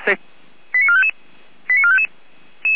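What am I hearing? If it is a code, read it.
Five-tone selective call (ZVEI), heard as quick runs of five tones, tones of about 70 ms, with speech. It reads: 89455 89455
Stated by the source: ZVEI five-tone selective call received on an SDR, from a fire brigade.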